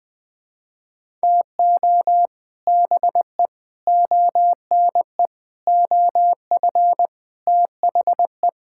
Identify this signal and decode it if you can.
Morse code, heard as a keyed tone, about 20 wpm, 700 Hz. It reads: TO BE ONE OF THE
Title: Sets of 6 Words - 20wpm